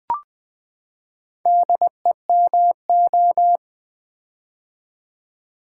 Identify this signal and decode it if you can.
Morse code, heard as a keyed tone, about 20 wpm, 700 Hz. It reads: DEMO
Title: Build A Word - 4 Letters At A Time - 11-Letter Words - 20wpm